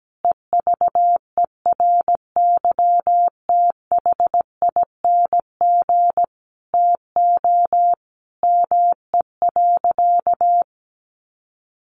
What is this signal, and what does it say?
Morse code, heard as a keyed tone, about 17 wpm, 700 Hz. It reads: EVERYTHING TO ME.
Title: Sentences from Top 700 Words 17wpm